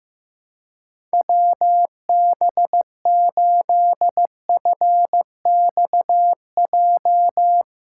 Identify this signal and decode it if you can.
Morse code, heard as a keyed tone, about 15 wpm, 700 Hz. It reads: WB8FXJ
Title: K1USN SST QSO Practice - 15wpm